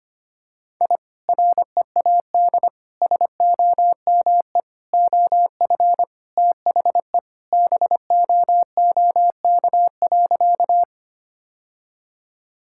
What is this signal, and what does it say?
Morse code, heard as a keyed tone, about 25 wpm, 700 Hz. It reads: I READ SOME OF THE BOOK.
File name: practice / 25wpm / Sentences with 2-Word Phrases - 25wpm